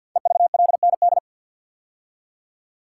Morse code, heard as a keyed tone, about 45 wpm, 700 Hz. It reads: EVLND
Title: Code Groups - 5 Letter:Numbers - 50wpm